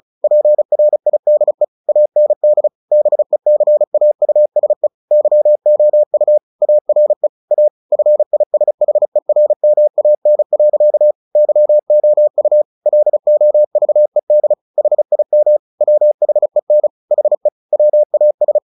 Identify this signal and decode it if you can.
Morse code, heard as a keyed tone, about 34 wpm, 600 Hz. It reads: PRIDE AND BECAUSE YOU ARE A FISHERMAN. YOU LOVED HIM WHEN HE WAS
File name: the old man and the sea 2 35wpm